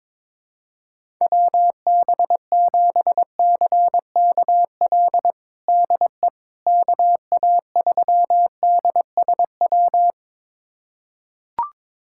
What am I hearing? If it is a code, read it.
Morse code, heard as a keyed tone, about 21 wpm, 700 Hz. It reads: WB7CKL DE KA3DSW
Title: USA Callsign DE Callsign - 6 Characters - 22wpm